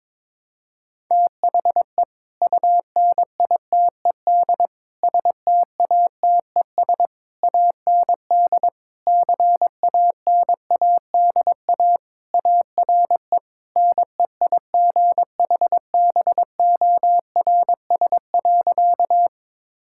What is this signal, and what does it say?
Morse code, heard as a keyed tone, about 22 wpm, 700 Hz. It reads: THE UNITED STATES AND CANADA ARE NEIGHBORS.
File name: practice / 22wpm / Sentences with 4-Word Phrases - 22wpm